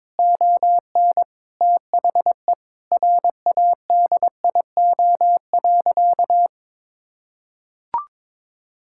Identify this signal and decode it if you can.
Morse code, heard as a keyed tone, about 22 wpm, 700 Hz. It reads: ON THE RADIO.
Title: Sentences from Top 1000 Words - Review Part 5 - 22wpm